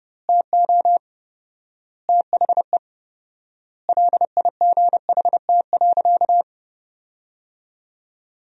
Morse code, heard as a keyed tone, about 30 wpm, 700 Hz. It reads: TO THE LIGHT.
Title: 4x Word-Spacing - Sentences from Top 400 Words - 30wpm